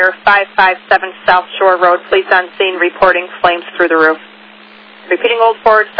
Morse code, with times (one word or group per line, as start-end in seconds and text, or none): none